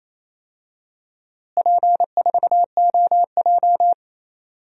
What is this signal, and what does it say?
Morse code, heard as a keyed tone, about 28 wpm, 700 Hz. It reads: P4OJ
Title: DXCC-100 Call Signs - 4 Characters - Repeated 3x - 28wpm